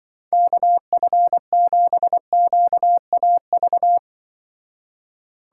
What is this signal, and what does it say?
Morse code, heard as a keyed tone, about 24 wpm, 700 Hz. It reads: KF7QAV
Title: Speed Racing - USA Call Signs 20wpm